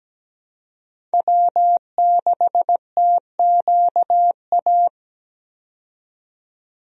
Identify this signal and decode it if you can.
Morse code, heard as a keyed tone, about 17 wpm, 700 Hz. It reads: W6TQA
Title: USA Call Signs - 5 Characters - 17wpm